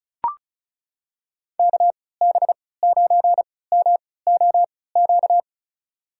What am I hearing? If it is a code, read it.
Morse code, heard as a keyed tone, about 36 wpm, spaced as for 20 wpm, 700 Hz. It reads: KB9MOQ
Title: USA Call Signs - 6 Characters - 3x Character Spacing - 35wpm